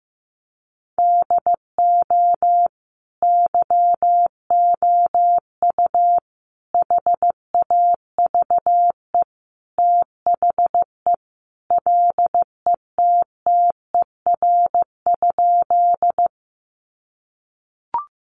Morse code, heard as a keyed tone, about 15 wpm, 700 Hz. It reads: DO YOU HAVE THE LETTER?
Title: Sentences from Top 300 Words 15wpm